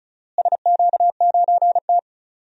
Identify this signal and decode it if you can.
Morse code, heard as a keyed tone, about 36 wpm, 700 Hz. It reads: SQ9T
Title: DXCC-100 Call Signs - 4 Characters - Repeated 3x - 35wpm